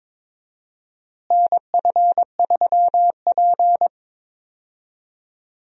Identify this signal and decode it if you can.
Morse code, heard as a keyed tone, about 22 wpm, 700 Hz. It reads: NF3P
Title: USA Call Signs - 4 Characters - Repeated 2x - 22wpm